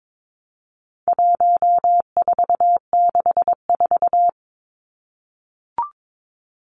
Morse code, heard as a keyed tone, about 22 wpm, 700 Hz. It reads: 1464
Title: Numbers - 4-Digit 22wpm